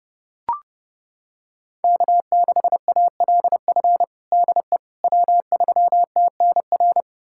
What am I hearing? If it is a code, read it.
Morse code, heard as a keyed tone, about 32 wpm, 700 Hz. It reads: K6ALF DE W3TNR
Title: USA Callsign DE Callsign - 5 Characters - 30wpm